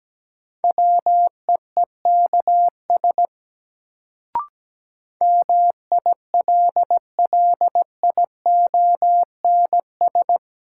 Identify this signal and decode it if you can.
Morse code, heard as a keyed tone, about 17 wpm, 700 Hz. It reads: WEEKS MILLIONS
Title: Sets of 5 Words - New Word Drill - 17wpm